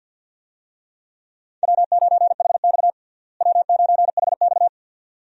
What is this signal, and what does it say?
Morse code, heard as a keyed tone, about 46 wpm, 700 Hz. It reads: W9HX W9HX
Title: USA Call Signs - 4 Characters - Repeated 2x - 50wpm